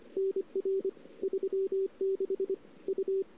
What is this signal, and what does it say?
Morse code, heard as a keyed tone, about 30 wpm, 380 Hz. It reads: NR 36 U
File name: CW / 2008-ss-cw-k5tr-op-n5rz -00:00